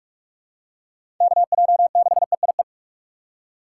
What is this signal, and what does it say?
Morse code, heard as a keyed tone, about 46 wpm, 700 Hz. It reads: KJ6EIE
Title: USA Call Signs - 6 Characters - 3x Character Spacing - 45wpm